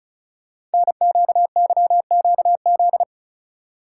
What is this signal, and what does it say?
Morse code, heard as a keyed tone, about 36 wpm, 690 Hz. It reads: NQYQZ
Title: Code Groups - 5 Letters - 35wpm